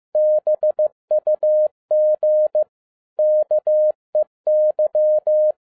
Morse code, heard as a keyed tone, about 15 wpm, 610 Hz. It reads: BUG KEY